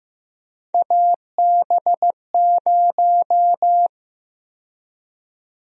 Morse code, heard as a keyed tone, about 15 wpm, 700 Hz. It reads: AB0
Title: USA Call Sign Prefixes - Repeated 2x - 15wpm